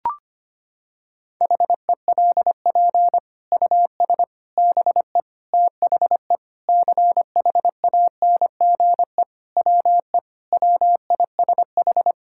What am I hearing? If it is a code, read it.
Morse code, heard as a keyed tone, about 25 wpm, 700 Hz. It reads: HELP US BE THE CHANGE WE WISH
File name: practice / 25wpm / Sentences with 9-Word Phrases - 25wpm